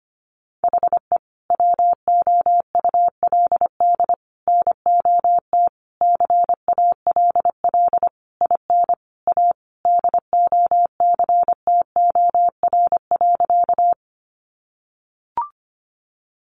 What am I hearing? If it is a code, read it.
Morse code, heard as a keyed tone, about 25 wpm, 700 Hz. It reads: HE WOULD NOT CALL IN A DOCTOR.
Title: Sentences from Top 1500 Words 25wpm